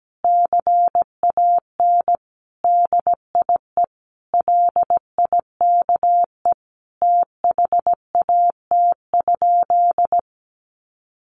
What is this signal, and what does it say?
Morse code, heard as a keyed tone, about 17 wpm, 700 Hz. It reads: CAN DIE LIKE THAT?
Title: Sentences from Top 1400 Words 17wpm